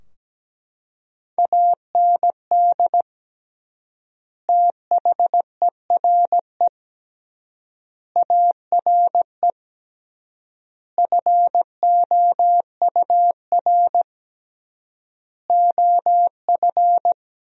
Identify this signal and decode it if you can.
Morse code, heard as a keyed tone, about 17 wpm, 700 Hz. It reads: AND THERE ARE FOUR OF
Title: 3x Word-Spacing - Sentences from Top 300 Words - 17wpm